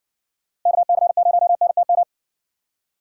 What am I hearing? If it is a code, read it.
Morse code, heard as a keyed tone, about 59 wpm, 700 Hz. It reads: KC0NTM